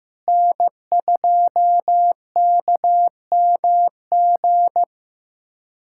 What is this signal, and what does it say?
Morse code, heard as a keyed tone, about 15 wpm, 700 Hz. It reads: N2KMG
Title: USA Callsign DE Callsign - 5 Characters - 15wpm